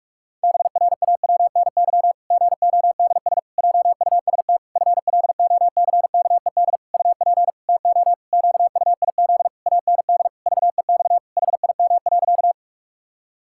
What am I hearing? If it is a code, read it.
Morse code, heard as a keyed tone, about 41 wpm, 700 Hz. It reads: BRAWNY GODS JUST FLOCKED UP TO QUIZ AND VEX HIM.
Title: Pangram Challenge v2 - 45wpm